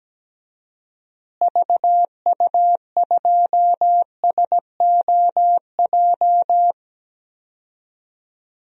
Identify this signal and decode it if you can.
Morse code, heard as a keyed tone, about 17 wpm, 700 Hz. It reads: VU2SOJ
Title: DXCC-100 Call Signs - 6 Characters - Repeated 2x - 17wpm